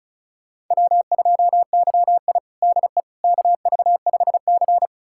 Morse code, heard as a keyed tone, about 36 wpm, 700 Hz. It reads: W2YI DE KV5C
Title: USA Callsign DE Callsign - 4 Characters - 35wpm